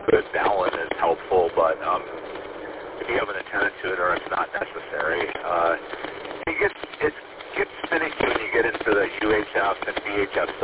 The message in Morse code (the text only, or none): none